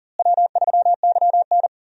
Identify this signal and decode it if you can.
Morse code, heard as a keyed tone, about 39 wpm, 700 Hz. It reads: W3YN
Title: USA Call Signs - 4 Characters - Repeated 3x - 40wpm